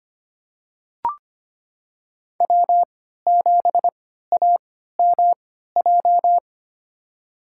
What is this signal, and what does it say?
Morse code, heard as a keyed tone, about 25 wpm, spaced as for 14 wpm, 700 Hz. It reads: W7AMJ
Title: USA Call Signs - 5 Characters - 3x Character Spacing - 25wpm